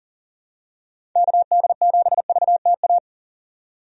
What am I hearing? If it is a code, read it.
Morse code, heard as a keyed tone, about 40 wpm, 700 Hz. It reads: KD7VTA
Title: USA Call Signs - 6 Characters - 40wpm